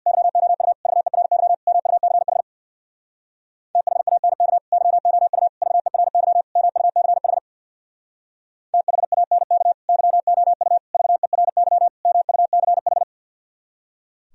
Morse code, heard as a keyed tone, about 64 wpm, 700 Hz. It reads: YOU VERY MUCH THANK YOU VERY MUCH THANK YOU VERY MUCH